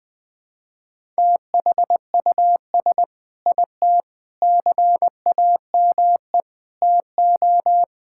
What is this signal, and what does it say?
Morse code, heard as a keyed tone, about 20 wpm, 700 Hz. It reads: THUS IT CAME TO
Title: Sentences from Top 1000 Words - Review Part 5 - 20wpm